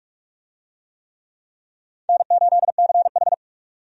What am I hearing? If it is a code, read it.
Morse code, heard as a keyed tone, about 46 wpm, 700 Hz. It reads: N8CH
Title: USA Call Signs - 4 Characters - 45wpm